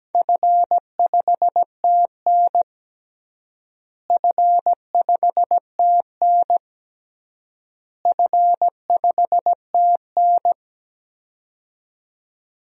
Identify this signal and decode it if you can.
Morse code, heard as a keyed tone, about 17 wpm, 700 Hz. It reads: F5TN F5TN F5TN
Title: DXCC-100 Call Signs - 4 Characters - Repeated 3x - 17wpm